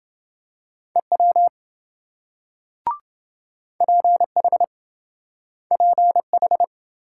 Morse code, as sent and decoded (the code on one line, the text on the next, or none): . .-- / .--. .... / .--. ....
EW PH PH